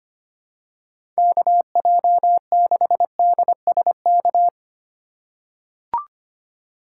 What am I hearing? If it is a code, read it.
Morse code, heard as a keyed tone, about 25 wpm, 700 Hz. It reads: KJ6DSK
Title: USA Call Signs - 3x Character Spacing - 25wpm